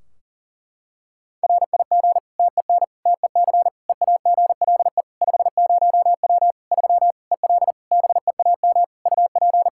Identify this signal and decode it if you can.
Morse code, heard as a keyed tone, about 40 wpm, 700 Hz. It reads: RIG TEN TEC EAGLE 50W 3 EL BEAM UP
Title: Practice QSO - Exchange 2 - Info - 40wpm